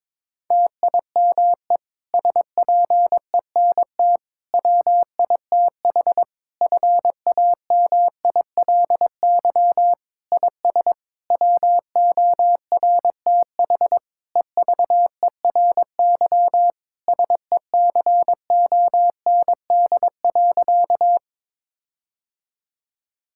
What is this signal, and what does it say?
Morse code, heard as a keyed tone, about 22 wpm, 700 Hz. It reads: TIME SPENT WITH FAMILY IS WORTH EVERY SECOND.
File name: Difficult Sentences with Hints - 22wpm